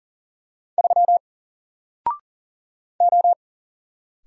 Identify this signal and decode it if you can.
Morse code, heard as a keyed tone, about 41 wpm, 700 Hz. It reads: 3 O
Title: Single Letter-Number Alt 40wpm